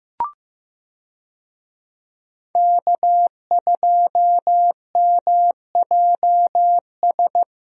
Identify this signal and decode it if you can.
Morse code, heard as a keyed tone, about 15 wpm, 700 Hz. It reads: K2MJS
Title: USA Call Signs - 5 Characters - Repeated 3x - 15wpm